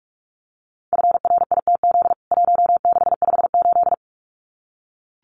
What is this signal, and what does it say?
Morse code, heard as a keyed tone, about 44 wpm, 700 Hz. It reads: FRITZ 1658
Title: CWT Exchange 45wpm